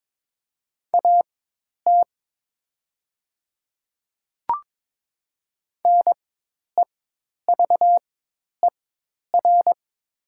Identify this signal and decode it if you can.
Morse code, heard as a keyed tone, about 22 wpm, spaced as for 10 wpm, 700 Hz. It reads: AT NEVER
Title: 4x Character Spacing - Top 300 Words - 22wpm